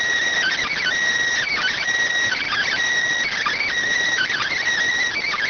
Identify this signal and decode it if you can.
Emergency Alert System (SAME) tones (header bursts), heard as harsh buzzing bursts. It is heard from 0.4 to 4.6 s.